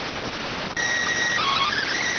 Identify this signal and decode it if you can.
SSTV (slow-scan television), PD 290.